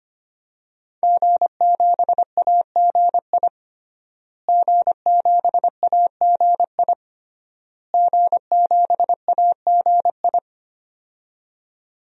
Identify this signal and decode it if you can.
Morse code, heard as a keyed tone, about 25 wpm, 700 Hz. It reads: G7AGI G7AGI G7AGI
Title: DXCC-100 Call Signs - 5 Characters - Repeated 3x - 25wpm